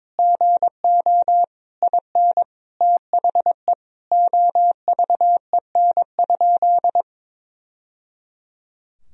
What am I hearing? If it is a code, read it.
Morse code, heard as a keyed tone, about 22 wpm, 700 Hz. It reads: GO IN THE OVEN?
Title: Sentences from Top 1000 Words - Review Part 4 - 22wpm